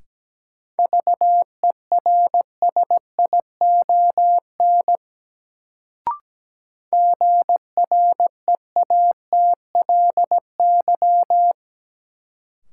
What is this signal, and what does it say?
Morse code, heard as a keyed tone, about 17 wpm, 700 Hz. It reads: VERSION GREATLY
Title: Sentences from Top 1400 Words - New Word Drill Alt - 17wpm